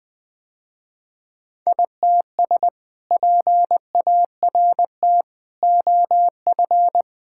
Morse code, heard as a keyed tone, about 21 wpm, 700 Hz. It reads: ITS PART OF
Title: Sentences with 3-Word Phrases - Spoken Beginning - 20wpm